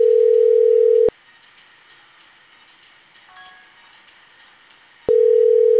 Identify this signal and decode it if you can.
Ringback tone (North America), 440 + 480 Hz.